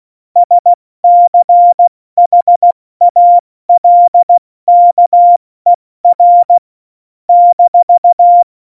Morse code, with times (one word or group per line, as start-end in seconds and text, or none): 0.4-6.6 SCHALKER
7.3-8.4 -